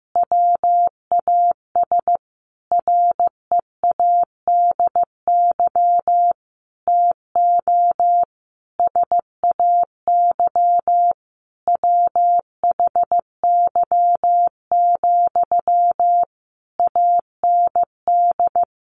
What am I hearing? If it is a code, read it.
Morse code, heard as a keyed tone, about 15 wpm, 700 Hz. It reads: WAS READY TO SAY WHY, AND
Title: Sentences from Top 1300 Words 15wpm